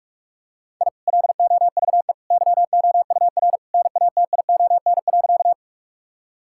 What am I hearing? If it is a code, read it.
Morse code, heard as a keyed tone, about 44 wpm, 700 Hz. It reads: I LOVE YOUR NATION.
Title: Sentences from Top 1000 Words - Review Part 6 - 45wpm